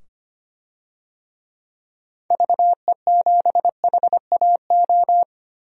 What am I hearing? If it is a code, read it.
Morse code, heard as a keyed tone, about 25 wpm, 700 Hz. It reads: VE7HAO
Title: DXCC-100 Call Signs - 6 Characters - 25wpm